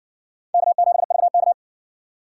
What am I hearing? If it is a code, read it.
Morse code, heard as a keyed tone, about 60 wpm, 700 Hz. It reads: K7VK